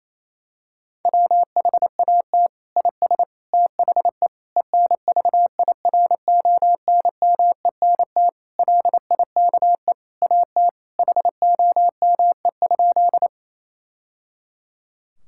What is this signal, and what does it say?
Morse code, heard as a keyed tone, about 28 wpm, 700 Hz. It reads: WHAT IS THE ENVIRONMENT LIKE AT HOME?
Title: Sentences from Top 1000 Words - Review Part 5 - 28wpm